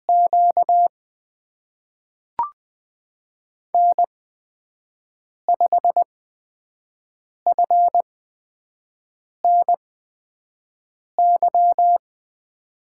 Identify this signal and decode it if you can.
Morse code, heard as a keyed tone, about 20 wpm, spaced as for 5 wpm, 700 Hz. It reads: Q N5FNY
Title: USA Call Signs - 5 Characters - 8x Character Spacing - 20wpm